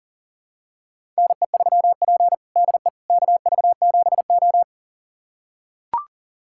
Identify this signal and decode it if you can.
Morse code, heard as a keyed tone, about 40 wpm, 700 Hz. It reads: NE3P DE KV7O
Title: USA Callsign DE Callsign - 4 Characters - 40wpm